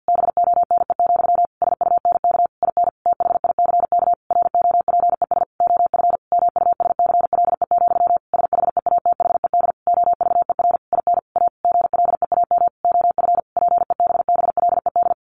Morse code, heard as a keyed tone, about 48 wpm, 700 Hz. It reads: BONE, SUNK IN THICK ROPES OF MUSCLE, SHEATHED OVER IN A GLEAM OF PEBBLED
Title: A Sound Of Thunder 50wpm